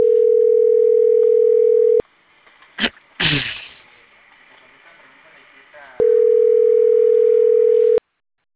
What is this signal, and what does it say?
Ringback tone (North America), 440 + 480 Hz, with speech.
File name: CENTRAL CAEA / out agcaea02 - .16755